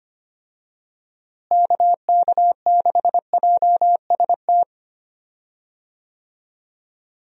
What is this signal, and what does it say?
Morse code, heard as a keyed tone, about 25 wpm, 700 Hz. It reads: KK6JST
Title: USA Call Signs - 6 Characters - Repeated 2x - 25wpm